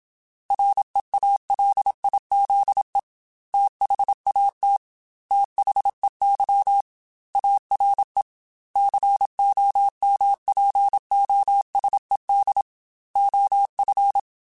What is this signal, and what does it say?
Morse code, heard as a keyed tone, about 27 wpm, 800 Hz. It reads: REALIZE THAT THEY ARE COMPOSED OF
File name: Technical Subjects / Technical Subjects-mp3s / Sw Pwr Sup @25